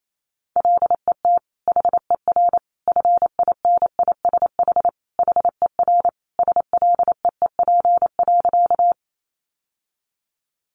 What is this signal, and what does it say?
Morse code, heard as a keyed tone, about 28 wpm, 700 Hz. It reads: LET HER FINISH HER SLEEP.
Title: Sentences from Top 1200 Words 28wpm